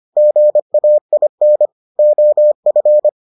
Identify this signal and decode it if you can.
Morse code, heard as a keyed tone, about 25 wpm, 600 Hz. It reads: GAIN OF